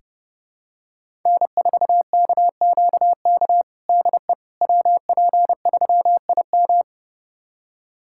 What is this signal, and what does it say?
Morse code, heard as a keyed tone, about 30 wpm, 700 Hz. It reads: N4KQK DE WP3IM
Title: USA Callsign DE Callsign - 5 Characters - 30wpm